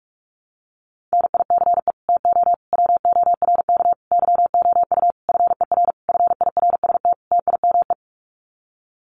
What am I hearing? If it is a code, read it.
Morse code, heard as a keyed tone, about 44 wpm, 700 Hz. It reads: NICE TO WORK YOU FER FIRST TIME
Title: Common QSO Phrases 45wpm